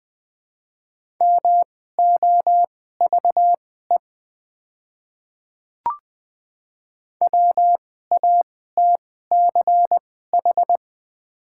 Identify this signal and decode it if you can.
Morse code, heard as a keyed tone, about 20 wpm, spaced as for 14 wpm, 700 Hz. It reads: MOVE WATCH
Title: 2x Character Spacing - Top 300 Words - 20wpm